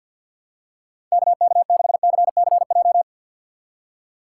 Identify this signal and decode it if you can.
Morse code, heard as a keyed tone, about 53 wpm, 700 Hz. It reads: KK6CCJ
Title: USA Call Signs - 6 Characters - 4x Character Spacing - 50wpm